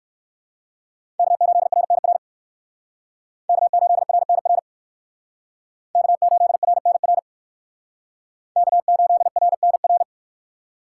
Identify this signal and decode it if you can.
Morse code, heard as a keyed tone, about 58 wpm, 700 Hz. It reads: K8RNR K8RNR K8RNR K8RNR